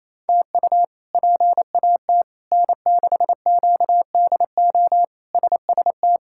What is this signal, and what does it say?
Morse code, heard as a keyed tone, about 28 wpm, 700 Hz. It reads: TU PAT N6QDO SST
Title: K1USN SST QSO Practice - 28wpm